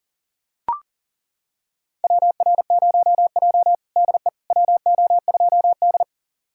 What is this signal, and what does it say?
Morse code, heard as a keyed tone, about 40 wpm, 700 Hz. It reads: WR0J DE WO2D
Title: USA Callsign DE Callsign - 4 Characters - 40wpm